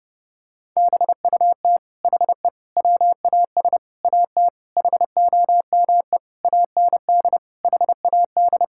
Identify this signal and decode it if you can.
Morse code, heard as a keyed tone, about 30 wpm, 700 Hz. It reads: BUT HE WAS AT HOME AND HAD
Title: Sentences from Top 1000 Words - Review Part 5 - 30wpm